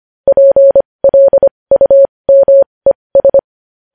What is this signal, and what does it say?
Morse code, heard as a keyed tone, about 25 wpm, spaced as for 20 wpm, 550 Hz. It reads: PLUMES
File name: podcast / News headlines-1x-25x20